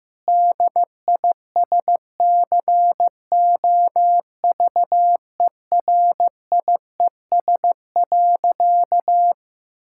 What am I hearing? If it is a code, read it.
Morse code, heard as a keyed tone, about 15 wpm, 700 Hz. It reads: DISCOVERIES.
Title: Pangram Challenge v2 - 15wpm